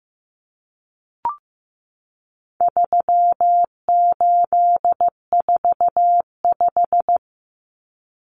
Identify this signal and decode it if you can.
Morse code, heard as a keyed tone, about 15 wpm, 700 Hz. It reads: 3845
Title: Common Cut Numbers - 4-Digit 15wpm